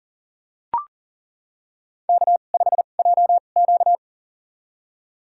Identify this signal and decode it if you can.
Morse code, heard as a keyed tone, about 40 wpm, spaced as for 29 wpm, 700 Hz. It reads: K5JQ
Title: USA Call Signs - 4 Characters - 2x Character Spacing - 40wpm